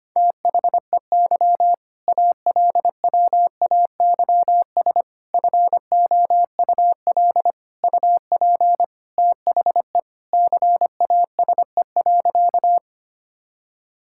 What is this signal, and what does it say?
Morse code, heard as a keyed tone, about 25 wpm, 700 Hz. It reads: THEY ALWAYS FOUL UP THE CASE.
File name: practice / 25wpm / All-Age Appropriate Jokes - 25wpm